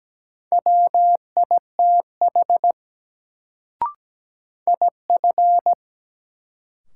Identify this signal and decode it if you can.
Morse code, heard as a keyed tone, about 17 wpm, 700 Hz. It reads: WITH IF
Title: Sets of 2 Words - New Word Drill - 17wpm